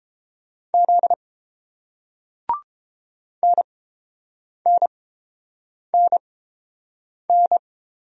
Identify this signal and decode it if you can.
Morse code, heard as a keyed tone, about 30 wpm, spaced as for 7 wpm, 700 Hz. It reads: Z NNNN